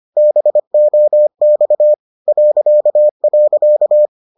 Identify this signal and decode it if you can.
Morse code, heard as a keyed tone, about 25 wpm, 600 Hz. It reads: BOX ..